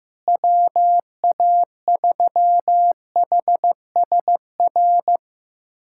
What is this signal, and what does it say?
Morse code, heard as a keyed tone, about 15 wpm, 700 Hz. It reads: WA3HSR
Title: USA Callsign DE Callsign 15wpm